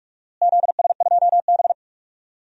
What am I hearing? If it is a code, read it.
Morse code, heard as a keyed tone, about 47 wpm, 700 Hz. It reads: ZS2B
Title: DXCC-100 Call Signs - 4 Characters - Repeated 2x - 45wpm